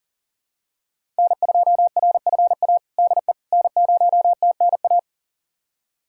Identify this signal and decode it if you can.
Morse code, heard as a keyed tone, about 40 wpm, 700 Hz. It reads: N2RFA DE N0TNA
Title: USA Callsign DE Callsign - 5 Characters - 40wpm